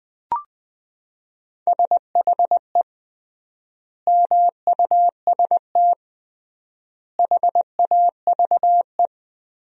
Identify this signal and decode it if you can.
Morse code, heard as a keyed tone, about 21 wpm, 700 Hz. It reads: SHE MUST HAVE
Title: 3x Word-Spacing - Sentences from Top 300 Words - 20wpm